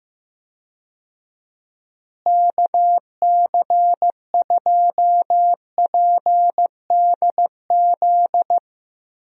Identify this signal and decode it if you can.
Morse code, heard as a keyed tone, about 15 wpm, 700 Hz. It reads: KC2PDZ
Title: USA Call Signs - 6 Characters - Repeated 3x - 15wpm